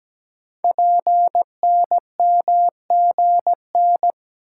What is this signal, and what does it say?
Morse code, heard as a keyed tone, about 17 wpm, 700 Hz. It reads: PNMGN
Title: Code Groups - 5 Letters - 17wpm